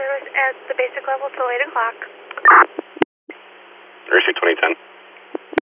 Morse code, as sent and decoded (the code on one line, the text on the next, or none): none